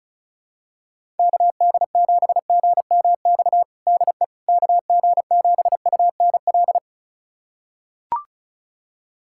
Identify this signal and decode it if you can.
Morse code, heard as a keyed tone, about 34 wpm, 700 Hz. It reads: KD7GMX DE KG7UNL
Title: USA Callsign DE Callsign - 6 Characters - 35wpm